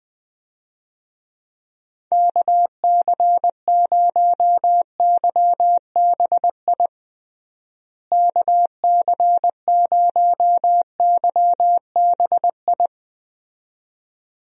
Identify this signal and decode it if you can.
Morse code, heard as a keyed tone, about 20 wpm, 700 Hz. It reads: KC0YBI KC0YBI
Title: USA Call Signs - 6 Characters - Repeated 2x - 20wpm